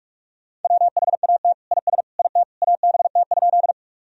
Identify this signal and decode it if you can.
Morse code, heard as a keyed tone, about 44 wpm, 700 Hz. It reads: WHAT IS IT ABT?
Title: Sentences from Top 100 Words and Abbreviations 45wpm